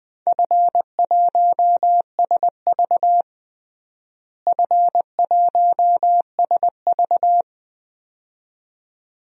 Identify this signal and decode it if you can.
Morse code, heard as a keyed tone, about 20 wpm, 700 Hz. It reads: F1SV F1SV
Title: DXCC-100 Call Signs - 4 Characters - Repeated 2x - 20wpm